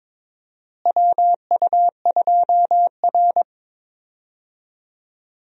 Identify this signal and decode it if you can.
Morse code, heard as a keyed tone, about 22 wpm, 700 Hz. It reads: WU2R
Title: USA Call Signs - 4 Characters - Repeated 2x - 22wpm